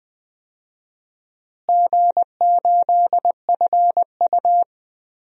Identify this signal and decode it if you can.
Morse code, heard as a keyed tone, about 20 wpm, 700 Hz. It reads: G8FU
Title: DXCC-100 Call Signs - 4 Characters - Repeated 3x - 20wpm